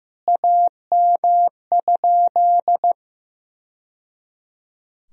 Morse code, heard as a keyed tone, about 14 wpm, 700 Hz. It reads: AM?
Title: Sentences from Top 1000 Words - Review Part 3 - 15wpm